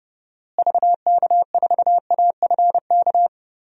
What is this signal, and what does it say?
Morse code, heard as a keyed tone, about 31 wpm, 700 Hz. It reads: VK4AFK
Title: DXCC-100 Call Signs - 6 Characters - Repeated 3x - 30wpm